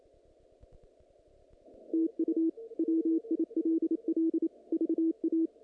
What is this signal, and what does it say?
Morse code, heard as a keyed tone, about 27 wpm, 330 Hz. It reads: TU WILL VA
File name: contests / naqp-cw-k5tr-op-k5pi- -2200